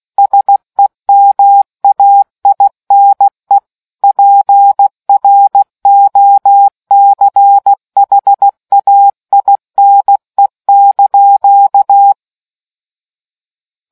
Morse code, heard as a keyed tone, about 16 wpm, 800 Hz. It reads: SEMAINE PROCHAINE)